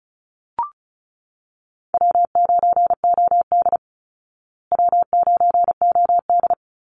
Morse code, heard as a keyed tone, about 34 wpm, 700 Hz. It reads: W9OD W9OD
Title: Call Signs Repeated 3x - 35wpm